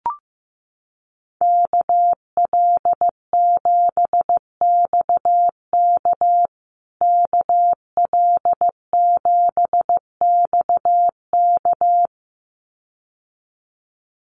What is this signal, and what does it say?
Morse code, heard as a keyed tone, about 15 wpm, 700 Hz. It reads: KL7XK KL7XK
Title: USA Call Signs Repeated 2x 15wpm